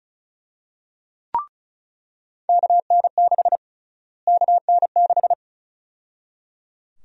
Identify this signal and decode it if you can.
Morse code, heard as a keyed tone, about 34 wpm, 700 Hz. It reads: KN6 KN6
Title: USA Call Sign Prefixes - Repeated 2x - 35wpm